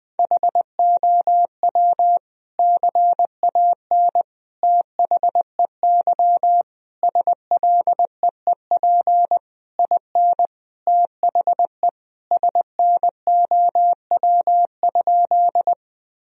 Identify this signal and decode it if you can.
Morse code, heard as a keyed tone, about 20 wpm, 700 Hz. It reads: HOW CAN THEY SLEEP IN THE SNOW?
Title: Sentences from Top 1000 Words - Review Part 3 - 20wpm